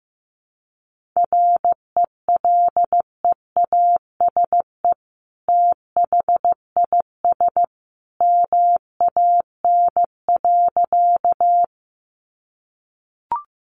Morse code, heard as a keyed tone, about 15 wpm, 700 Hz. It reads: RELEASE THIS MAN.
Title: Sentences from Top 1100 Words 15wpm